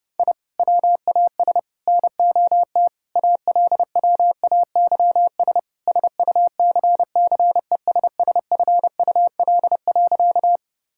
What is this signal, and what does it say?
Morse code, heard as a keyed tone, about 30 wpm, 700 Hz. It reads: I WAS NOT ALWAYS SUCCESSFUL.
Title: Sentences from Top 1000 Words - Review Part 6 - 30wpm